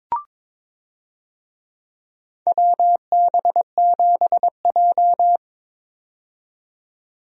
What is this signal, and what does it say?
Morse code, heard as a keyed tone, about 22 wpm, 700 Hz. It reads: WB7J
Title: USA Call Signs - 4 Characters - 22wpm